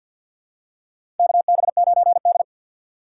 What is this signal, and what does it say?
Morse code, heard as a keyed tone, about 52 wpm, 700 Hz. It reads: KB9D
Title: USA Call Signs - 4 Characters - 3x Character Spacing - 50wpm